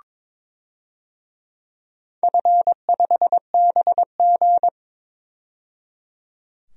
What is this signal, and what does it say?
Morse code, heard as a keyed tone, about 22 wpm, 700 Hz. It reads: F5BG
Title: DXCC-100 Call Signs - 4 Characters - 22wpm